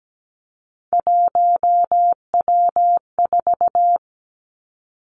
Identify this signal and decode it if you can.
Morse code, heard as a keyed tone, about 17 wpm, 700 Hz. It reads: 1W4
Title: Three Challenging Letter-Number Combinations 17wpm